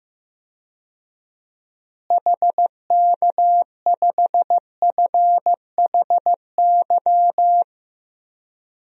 Morse code, heard as a keyed tone, about 15 wpm, 700 Hz. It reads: HK5FHY
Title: DXCC-100 Call Signs - 6 Characters - Repeated 2x - 15wpm